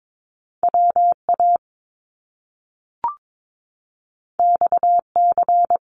Morse code, heard as a keyed tone, about 22 wpm, 700 Hz. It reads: WA XC
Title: Top Two Letter Combinations in Words 22wpm